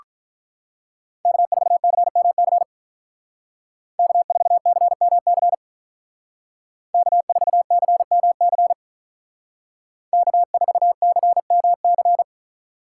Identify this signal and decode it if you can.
Morse code, heard as a keyed tone, about 46 wpm, 700 Hz. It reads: K4CMC K4CMC K4CMC K4CMC